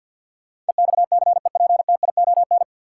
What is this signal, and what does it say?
Morse code, heard as a keyed tone, about 51 wpm, 700 Hz. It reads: EXCEPTION
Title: Sets of 4 Words - New Word Drill - 50wpm